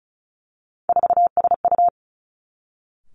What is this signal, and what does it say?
Morse code, heard as a keyed tone, about 36 wpm, 700 Hz. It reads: 4SU
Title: Three Challenging Letter-Number Combinations 35wpm